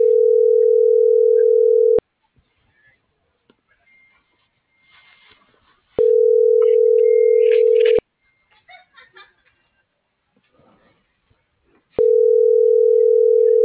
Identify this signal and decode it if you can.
Ringback tone (North America), 440 + 480 Hz.